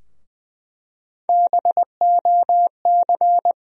Morse code, heard as a keyed tone, about 20 wpm, 700 Hz. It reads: BOC
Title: USA Call Sign Postfixes - 3 Letters 20wpm